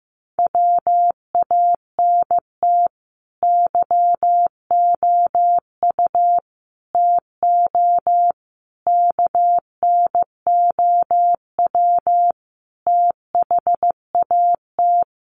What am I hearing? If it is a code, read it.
Morse code, heard as a keyed tone, about 15 wpm, 700 Hz. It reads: WANT YOU TO KNOW THAT
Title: Sets of 5 Words Encore 15wpm